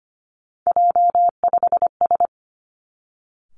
Morse code, heard as a keyed tone, about 25 wpm, 700 Hz. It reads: J5S
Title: Three Challenging Letter-Number Combinations 25wpm